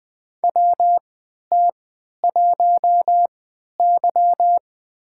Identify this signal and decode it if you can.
Morse code, heard as a keyed tone, about 20 wpm, spaced as for 11 wpm, 700 Hz. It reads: WT1Y
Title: USA Call Signs - 4 Characters - 3x Character Spacing - 20wpm